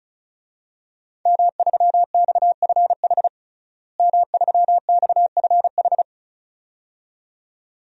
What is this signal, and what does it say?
Morse code, heard as a keyed tone, about 34 wpm, 700 Hz. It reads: M3XFH M3XFH
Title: DXCC-100 Call Signs - 5 Characters - Repeated 2x - 35wpm